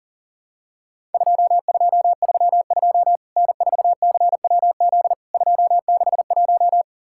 Morse code, heard as a keyed tone, about 37 wpm, 700 Hz. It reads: 2232 N4CWZ 261
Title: Worked All Europe DX Contest - QTC Exchange - 40wpm